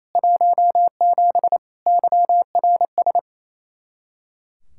Morse code, heard as a keyed tone, about 34 wpm, 700 Hz. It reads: 17 YRS
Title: Speed-Racing Common QSO Phrases Encore - 28wpm